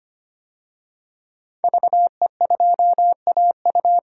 Morse code, heard as a keyed tone, about 25 wpm, 700 Hz. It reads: VE2AU
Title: DXCC-100 Call Signs - 5 Characters - 25wpm